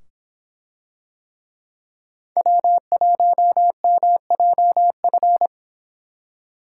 Morse code, heard as a keyed tone, about 24 wpm, 700 Hz. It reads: W1MJF